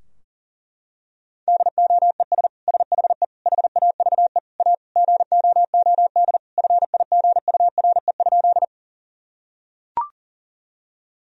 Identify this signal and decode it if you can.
Morse code, heard as a keyed tone, about 37 wpm, 700 Hz. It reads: DOES SHE HAVE A GOOD FIGURE?
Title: Sentences from Top 1000 Words - Review Part 2 - 40wpm